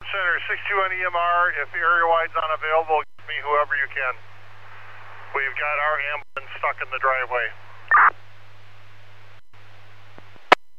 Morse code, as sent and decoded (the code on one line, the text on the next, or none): none